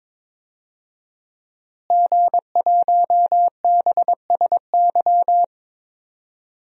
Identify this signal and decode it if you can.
Morse code, heard as a keyed tone, about 22 wpm, 700 Hz. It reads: G1BSY
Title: DXCC-100 Call Signs - 5 Characters - Repeated 3x - 22wpm